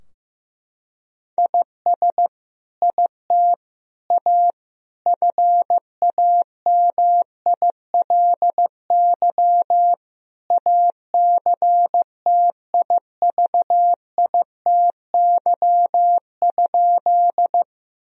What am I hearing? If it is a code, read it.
Morse code, heard as a keyed tone, about 15 wpm, 700 Hz. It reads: IS IT A FAMILY ACTIVITY?
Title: Sentences from Top 1000 Words - Review Part 4 - 15wpm